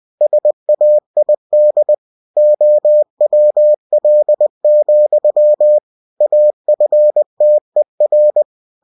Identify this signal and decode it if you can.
Morse code, heard as a keyed tone, about 20 wpm, 600 Hz. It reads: SAID OWL, AFTER
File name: cw-books / winnie-the-pooh / winnie-20wpm-page-0098